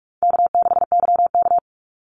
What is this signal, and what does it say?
Morse code, heard as a keyed tone, about 44 wpm, 700 Hz. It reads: K6YK
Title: NA SOTA Chaser Callsigns 45wpm